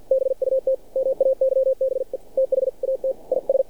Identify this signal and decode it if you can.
Morse code, heard as a keyed tone, about 47 wpm, 530 Hz. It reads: BUT MAYBE THAT IS
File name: archived / CRSnet